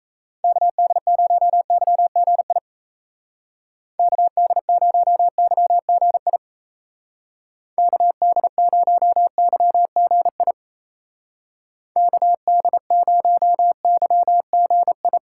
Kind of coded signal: Morse code

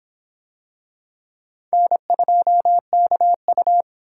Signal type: Morse code

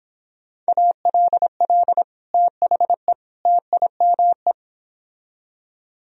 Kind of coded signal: Morse code

